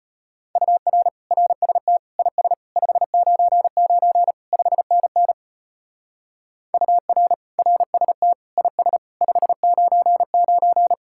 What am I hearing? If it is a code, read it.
Morse code, heard as a keyed tone, about 31 wpm, 700 Hz. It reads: UR RST IS 599 5NN UR RST IS 599
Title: Speed-Racing Common QSO Phrases Encore - 25wpm